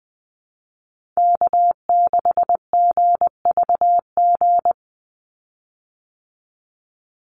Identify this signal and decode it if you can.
Morse code, heard as a keyed tone, about 20 wpm, 700 Hz. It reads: K6GVG
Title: Call Signs Repeated 3x - 20wpm